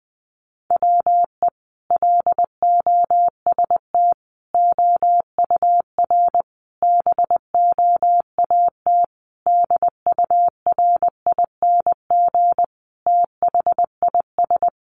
Morse code, heard as a keyed tone, about 20 wpm, 700 Hz. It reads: WE LOST OUR BOAT DURING THIS
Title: Sentences from Top 1500 Words 20wpm